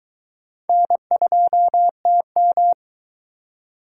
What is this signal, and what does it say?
Morse code, heard as a keyed tone, about 21 wpm, 700 Hz. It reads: N2TM